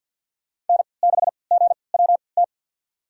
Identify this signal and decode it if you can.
Morse code, heard as a keyed tone, about 48 wpm, spaced as for 28 wpm, 700 Hz. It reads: N6GWT
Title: USA Call Signs - 5 Characters - 3x Character Spacing - 50wpm